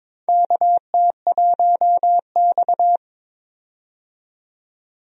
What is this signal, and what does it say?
Morse code, heard as a keyed tone, about 22 wpm, 700 Hz. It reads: KT1X
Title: USA Call Signs - 4 Characters - 22wpm